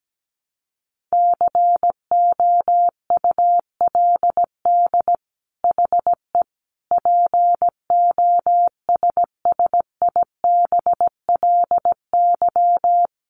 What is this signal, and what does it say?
Morse code, heard as a keyed tone, about 17 wpm, 700 Hz. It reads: COULD HE POSSIBLY
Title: Sentences from Top 1300 Words 17wpm